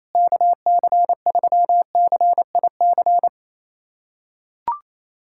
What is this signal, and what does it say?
Morse code, heard as a keyed tone, about 28 wpm, 700 Hz. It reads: KC3CIC
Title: USA Callsign DE Callsign - 6 Characters - 28wpm